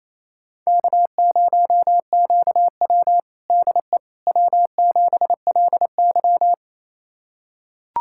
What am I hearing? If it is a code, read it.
Morse code, heard as a keyed tone, about 27 wpm, 700 Hz. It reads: K0QW DE W7LY
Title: USA Callsign DE Callsign - 4 Characters - 28wpm